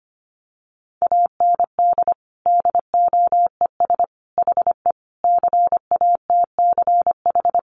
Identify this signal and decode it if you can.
Morse code, heard as a keyed tone, about 25 wpm, 700 Hz. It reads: AND DOES HE CATCH
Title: Sentences from Top 1500 Words 25wpm